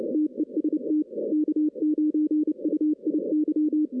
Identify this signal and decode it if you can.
Morse code, heard as a keyed tone, about 30 wpm, 300 Hz. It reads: TEST K9UIY
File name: 2016 arrl dx cw / 2016-ARRL-DX-CW-K5TR- - 0-34